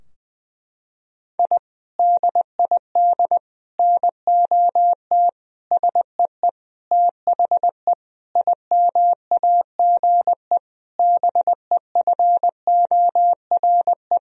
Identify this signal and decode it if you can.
Morse code, heard as a keyed tone, about 20 wpm, 700 Hz. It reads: I DID NOT SEE THE IMAGE BEFORE
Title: Sentences from Top 1000 Words - Review Part 4 - 20wpm